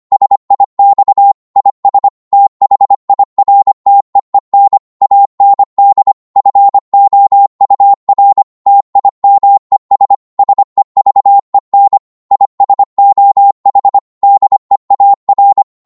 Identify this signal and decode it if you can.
Morse code, heard as a keyed tone, about 25 wpm, 800 Hz. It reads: SIX IS THIRTEEN AND FOUR TIMES SEVEN IS OH DEAR